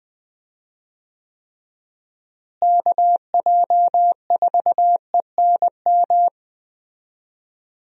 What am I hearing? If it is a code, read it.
Morse code, heard as a keyed tone, about 21 wpm, 700 Hz. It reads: KJ4ENM
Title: USA Call Signs - 6 Characters - Repeated 2x - 20wpm